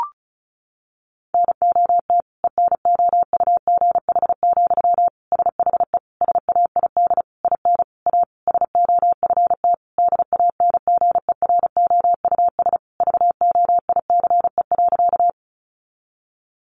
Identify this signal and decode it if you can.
Morse code, heard as a keyed tone, about 34 wpm, 700 Hz. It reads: NOT ENOUGH, SHE SAID IN A SOFT DANGEROUS VOICE.
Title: Sentences from Top 1500 Words 35wpm